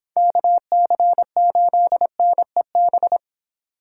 Morse code, heard as a keyed tone, about 23 wpm, 700 Hz. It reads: KC8NEB